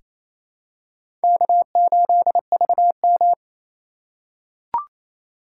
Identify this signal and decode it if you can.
Morse code, heard as a keyed tone, about 28 wpm, 700 Hz. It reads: K8VM
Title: USA Call Signs - 4 Characters - 3x Character Spacing - 28wpm